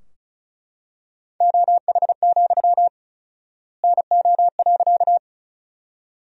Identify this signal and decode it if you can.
Morse code, heard as a keyed tone, about 34 wpm, 700 Hz. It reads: OH, NO.
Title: 4x Word-Spacing - Sentences from Top 500 Words - 35wpm